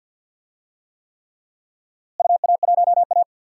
Morse code, heard as a keyed tone, about 48 wpm, 700 Hz. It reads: UA1A
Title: DXCC-100 Call Signs - 4 Characters - Repeated 3x - 50wpm